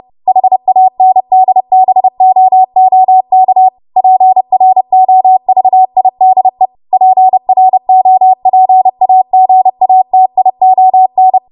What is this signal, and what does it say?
Morse code, heard as a keyed tone, about 30 wpm, 750 Hz. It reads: HANDBOOK PROVIDE PROPAGATION
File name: Morse / 30WPM